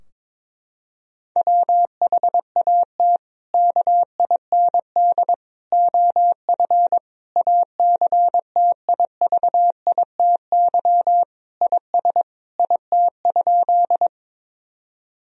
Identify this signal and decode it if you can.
Morse code, heard as a keyed tone, about 22 wpm, 700 Hz. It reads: WHAT KIND OF ACTIVITY IS IT?
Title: Sentences from Top 1000 Words - Review Part 4 - 22wpm